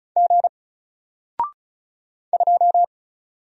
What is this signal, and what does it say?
Morse code, heard as a keyed tone, about 34 wpm, 700 Hz. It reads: G 2
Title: Single Letter-Number Alt 35wpm